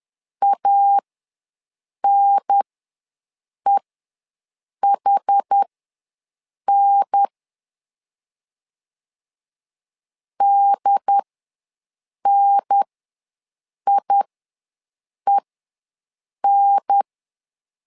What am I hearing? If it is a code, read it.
Morse code, heard as a keyed tone, about 10 wpm, spaced as for 6 wpm, 800 Hz. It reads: ANEHN DNIEN
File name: 2 UHDN ws 10wpm